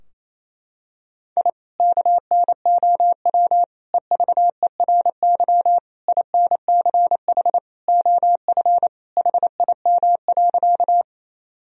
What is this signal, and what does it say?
Morse code, heard as a keyed tone, about 28 wpm, 700 Hz. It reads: I KNOW EVERY INCH OF HIM.
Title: Sentences from Top 1000 Words - Review Part 3 - 28wpm